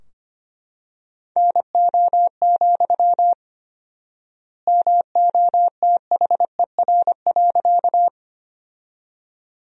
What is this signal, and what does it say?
Morse code, heard as a keyed tone, about 25 wpm, 700 Hz. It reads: NO, MOTHER.
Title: 4x Word-Spacing - Sentences from Top 200 Words - 25wpm